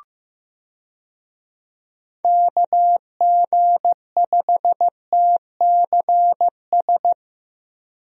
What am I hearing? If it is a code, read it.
Morse code, heard as a keyed tone, about 15 wpm, 700 Hz. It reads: KG5TCS
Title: USA Call Signs - 6 Characters - Repeated 2x - 15wpm